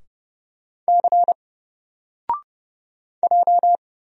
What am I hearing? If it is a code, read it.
Morse code, heard as a keyed tone, about 30 wpm, 700 Hz. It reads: C J
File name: Single-Letter-Number-30wpm